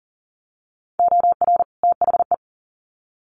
Morse code, heard as a keyed tone, about 40 wpm, 700 Hz. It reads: OR THE
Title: Sets of 2 Words 40wpm